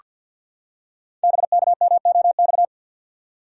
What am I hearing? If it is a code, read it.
Morse code, heard as a keyed tone, about 48 wpm, 690 Hz. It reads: BKMOX
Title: Code Groups - 5 Letters - 50wpm